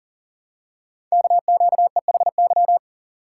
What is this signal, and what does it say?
Morse code, heard as a keyed tone, about 38 wpm, 700 Hz. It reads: KQEHY
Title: Code Groups - 5 Letters - 40wpm